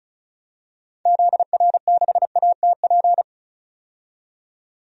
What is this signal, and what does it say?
Morse code, heard as a keyed tone, about 36 wpm, 700 Hz. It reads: ZR6ATP
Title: DXCC-100 Call Signs - 6 Characters - 35wpm